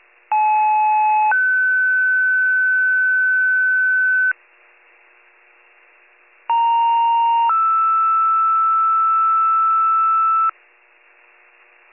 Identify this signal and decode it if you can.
Two-tone page, heard as a one-second tone, then a three-second tone, about 860 then 1500 Hz, about 930 then 1320 Hz. It is heard from 0.3 to 10.5 s.